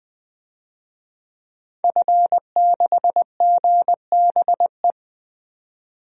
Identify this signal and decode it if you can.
Morse code, heard as a keyed tone, about 20 wpm, 700 Hz. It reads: F6GBE
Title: DXCC-100 Call Signs - 5 Characters - 20wpm